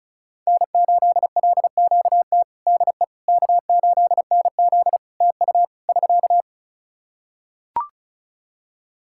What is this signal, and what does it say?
Morse code, heard as a keyed tone, about 36 wpm, 700 Hz. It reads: N8LQT DE K8NZ TU <SK>
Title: Practice QSO - Exchange 4 - Ending - 35wpm